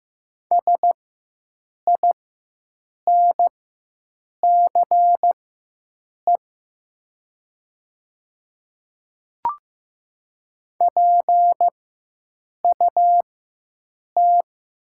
Morse code, heard as a keyed tone, about 15 wpm, spaced as for 7 wpm, 700 Hz. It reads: SINCE PUT
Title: 4x Character Spacing - Top 500 Words - 15wpm